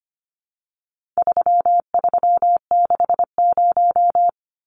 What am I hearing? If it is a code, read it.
Morse code, heard as a keyed tone, about 25 wpm, 700 Hz. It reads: 3360
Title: Numbers - 4-Digit 25wpm